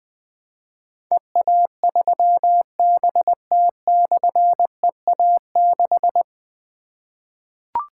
Morse code, heard as a keyed tone, about 20 wpm, 700 Hz. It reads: EA3BT/EA6
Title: Difficult Call Signs 20wpm